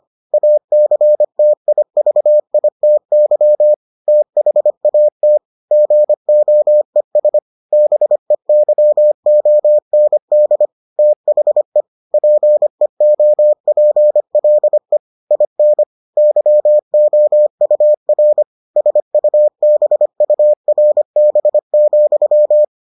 Morse code, heard as a keyed tone, about 25 wpm, 600 Hz. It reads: ACTIVITY THAT GOES BEYOND THE PEOPLE IN YOUR SUBURB,